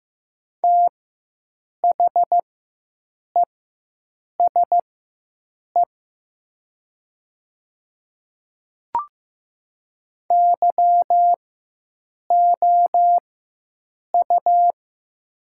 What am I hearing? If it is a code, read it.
Morse code, heard as a keyed tone, about 15 wpm, spaced as for 7 wpm, 700 Hz. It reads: THESE YOU
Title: 4x Character Spacing - Top 200 Words - 15wpm